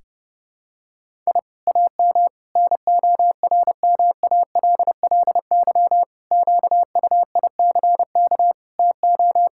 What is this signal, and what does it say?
Morse code, heard as a keyed tone, about 32 wpm, 700 Hz. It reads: I AM NORMALLY QUICK TO
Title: Sentences from Top 1000 Words - Review Part 6 - 30wpm